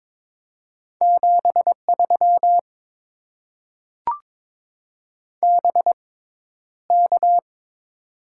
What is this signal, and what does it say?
Morse code, heard as a keyed tone, about 22 wpm, spaced as for 11 wpm, 700 Hz. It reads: 73 BK
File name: Top 90 QSO Elements - 6x Character Spacing - 22wpm